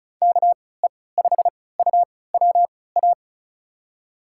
Morse code, heard as a keyed tone, about 34 wpm, spaced as for 20 wpm, 700 Hz. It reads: KE5UWA
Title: USA Call Signs - 6 Characters - 3x Character Spacing - 35wpm